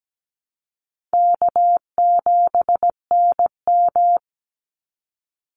Morse code, heard as a keyed tone, about 17 wpm, 700 Hz. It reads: K7NM
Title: Call Signs Repeated 2x - 17wpm